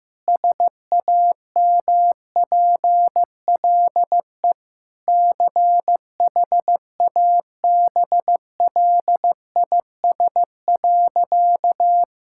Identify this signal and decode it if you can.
Morse code, heard as a keyed tone, about 15 wpm, 700 Hz. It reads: SAMPLE CHABLIS.
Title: Pangram Challenge v2 - 15wpm